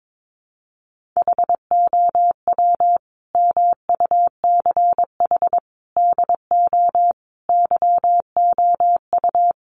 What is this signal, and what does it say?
Morse code, heard as a keyed tone, about 22 wpm, 700 Hz. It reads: HOW MUCH DO YOU
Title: Sentences from Top 200 Words 22wpm